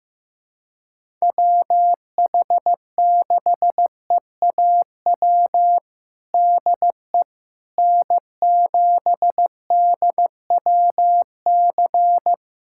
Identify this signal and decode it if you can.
Morse code, heard as a keyed tone, about 15 wpm, 700 Hz. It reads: WH6EAW DE N7DWC
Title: USA Callsign DE Callsign 15wpm